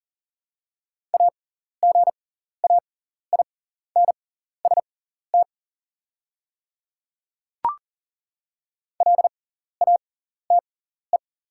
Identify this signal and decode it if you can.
Morse code, heard as a keyed tone, about 40 wpm, spaced as for 14 wpm, 700 Hz. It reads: AGAINST LATE
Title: 6x Character Spacing - Top 500 Words - 40wpm